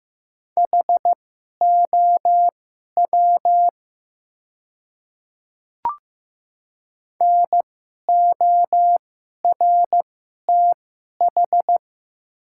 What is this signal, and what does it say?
Morse code, heard as a keyed tone, about 15 wpm, spaced as for 11 wpm, 700 Hz. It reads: HOW NORTH
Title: 2x Character Spacing - Top 300 Words - 15wpm